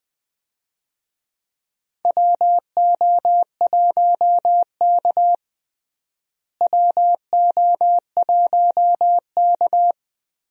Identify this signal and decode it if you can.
Morse code, heard as a keyed tone, about 20 wpm, 700 Hz. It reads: WO1K WO1K
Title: USA Call Signs - 4 Characters - Repeated 2x - 20wpm